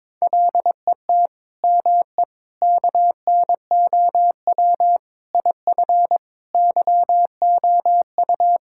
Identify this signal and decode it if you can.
Morse code, heard as a keyed tone, about 22 wpm, 700 Hz. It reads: LET ME KNOW IF YOU
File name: practice / 22wpm / Sets of 7 Words - 22wpm